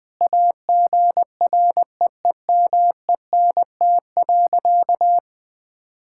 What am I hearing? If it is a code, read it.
Morse code, heard as a keyed tone, about 20 wpm, 700 Hz. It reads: AGREEMENT.
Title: Sentences with 5-Word Phrases - 20wpm